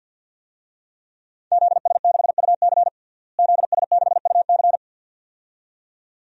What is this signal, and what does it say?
Morse code, heard as a keyed tone, about 50 wpm, 700 Hz. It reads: ZS6UC ZS6UC
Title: DXCC-100 Call Signs - 5 Characters - Repeated 2x - 50wpm